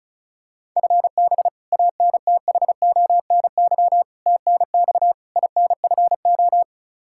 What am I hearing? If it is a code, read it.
Morse code, heard as a keyed tone, about 36 wpm, 700 Hz. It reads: FB ANTHONY TNX INFO
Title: K1USN SST QSO Practice 35wpm